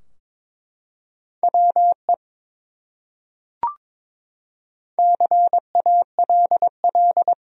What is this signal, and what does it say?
Morse code, heard as a keyed tone, about 22 wpm, 700 Hz. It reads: WE CALL
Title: Top 100 Words Repeated 22wpm